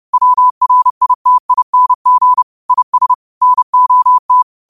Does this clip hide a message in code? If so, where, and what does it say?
Yes: Morse code from 0.1 to 4.4 s, heard as a keyed tone, about 30 wpm, 1000 Hz. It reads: WRITING IS NOT